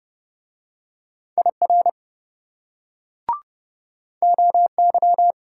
Morse code, heard as a keyed tone, about 28 wpm, 700 Hz. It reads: IR OY